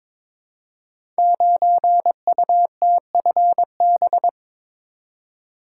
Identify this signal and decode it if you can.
Morse code, heard as a keyed tone, about 22 wpm, 700 Hz. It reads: 9UTFB
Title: Code Groups - 5 Letter:Numbers - 22wpm